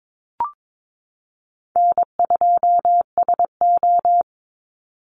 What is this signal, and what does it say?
Morse code, heard as a keyed tone, about 22 wpm, 700 Hz. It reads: N2SO
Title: Easy Call Signs Repeated 2x - 22wpm